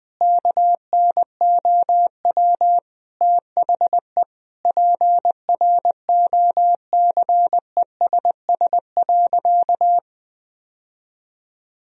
Morse code, heard as a keyed tone, about 21 wpm, 700 Hz. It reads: KNOW THE PROCESS.
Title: Sentences from Top 1000 Words - Review Part 3 - 20wpm